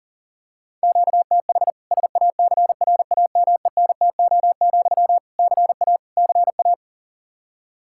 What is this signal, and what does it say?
Morse code, heard as a keyed tone, about 37 wpm, 700 Hz. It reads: QTH SACRAMENTO, CA CA
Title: Practice QSO - Exchange 1 - Beginning - 40wpm